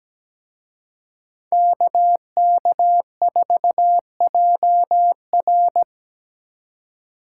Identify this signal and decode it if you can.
Morse code, heard as a keyed tone, about 20 wpm, 700 Hz. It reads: KK4JR
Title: Speed Racing - USA 2x2 Call Signs 17wpm